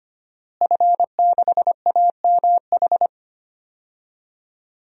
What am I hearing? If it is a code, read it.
Morse code, heard as a keyed tone, about 25 wpm, 700 Hz. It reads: F6AMH
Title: DXCC-100 Call Signs - 5 Characters - Repeated 2x - 25wpm